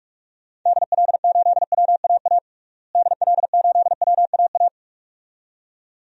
Morse code, heard as a keyed tone, about 41 wpm, 700 Hz. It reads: DL8WAA DL8WAA
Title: DXCC-100 Call Signs - 6 Characters - Repeated 2x - 45wpm